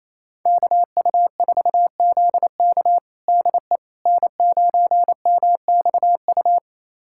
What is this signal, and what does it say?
Morse code, heard as a keyed tone, about 28 wpm, 700 Hz. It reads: KU4ZK DE N9MXU
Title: USA Callsign DE Callsign - 5 Characters - 28wpm